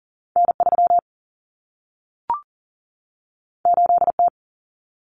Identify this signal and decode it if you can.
Morse code, heard as a keyed tone, about 40 wpm, 700 Hz. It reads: N3 8T
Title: Common Cut Numbers - 2-Digit 40wpm